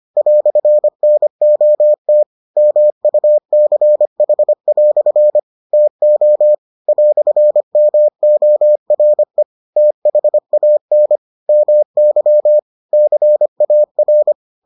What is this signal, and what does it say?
Morse code, heard as a keyed tone, about 25 wpm, 600 Hz. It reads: "NOT MUCH" TO "MORE THAN MY CAR